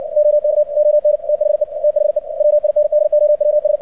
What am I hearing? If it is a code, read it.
Morse code, heard as a keyed tone, about 56 wpm, 600 Hz, with radio noise. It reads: OK ON THE THE WINDOWS
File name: archived / CRSnet